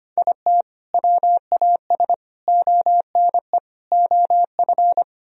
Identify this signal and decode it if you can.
Morse code, heard as a keyed tone, about 31 wpm, 700 Hz. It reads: IT WAS ONE OF
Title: Speed-Racing - Sets of 4 Words 25wpm